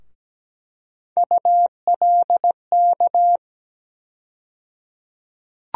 Morse code, heard as a keyed tone, about 17 wpm, 700 Hz. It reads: ULK
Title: USA Call Sign Postfixes - 3 Letters 17wpm